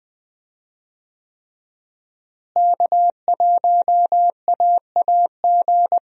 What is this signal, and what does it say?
Morse code, heard as a keyed tone, about 21 wpm, 700 Hz. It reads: K1AAG
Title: USA Call Signs - 5 Characters - Repeated 3x - 20wpm